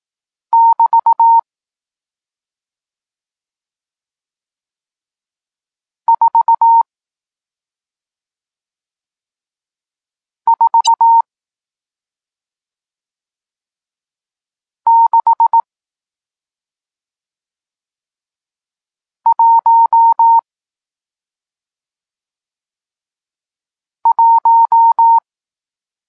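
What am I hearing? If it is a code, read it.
Morse code, heard as a keyed tone, about 18 wpm, spaced as for 2 wpm, 920 Hz. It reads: <BT> 44611
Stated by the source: Morse code lesson teaching digits 1, 4 and 6, from SP9MOA, Niepołomicki Klub Krótkofalowców (Poland) amateur radio club.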